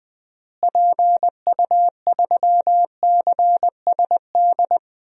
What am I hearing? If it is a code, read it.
Morse code, heard as a keyed tone, about 20 wpm, 700 Hz. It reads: PU3CSD
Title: DXCC-100 Call Signs - 6 Characters - Repeated 2x - 20wpm